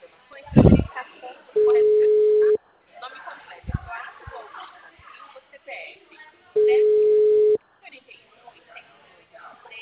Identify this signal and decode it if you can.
Ringback tone (Europe, 425 Hz), with speech.